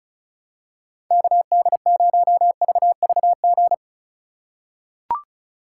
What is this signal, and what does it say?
Morse code, heard as a keyed tone, about 36 wpm, 700 Hz. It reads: KD0VVG
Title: USA Call Signs - 6 Characters - 2x Character Spacing - 35wpm